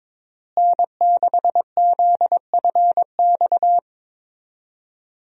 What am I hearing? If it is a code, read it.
Morse code, heard as a keyed tone, about 21 wpm, 700 Hz. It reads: N6ZFX
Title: USA Call Signs - 5 Characters - Repeated 3x - 22wpm